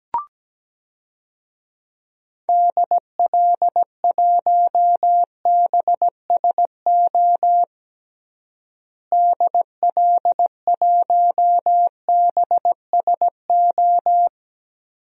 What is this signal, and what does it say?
Morse code, heard as a keyed tone, about 17 wpm, 700 Hz. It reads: DL1BSO DL1BSO
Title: DXCC-100 Call Signs - 6 Characters - Repeated 2x - 17wpm